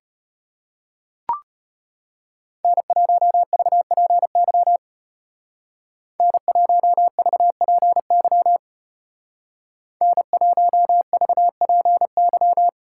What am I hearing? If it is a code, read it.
Morse code, heard as a keyed tone, about 32 wpm, 700 Hz. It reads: N1VPY N1VPY N1VPY